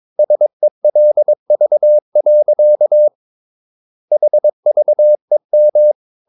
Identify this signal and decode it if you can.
Morse code, heard as a keyed tone, about 22 wpm, 600 Hz. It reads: SELV. HVEM
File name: Ferie QRS 22wpm